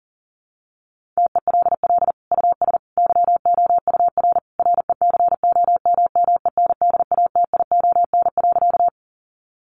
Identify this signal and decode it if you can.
Morse code, heard as a keyed tone, about 40 wpm, 700 Hz. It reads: TELL US YOUR RECOMMENDATION.
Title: Sentences from Top 1400 Words 40wpm